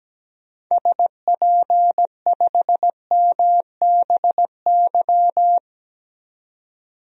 Morse code, heard as a keyed tone, about 17 wpm, 700 Hz. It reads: SP5MBY
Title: DXCC-100 Call Signs - 6 Characters - 17wpm